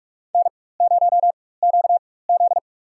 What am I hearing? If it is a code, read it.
Morse code, heard as a keyed tone, about 46 wpm, spaced as for 21 wpm, 700 Hz. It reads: N0QZ
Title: USA Call Signs - 4 Characters - 4x Character Spacing - 45wpm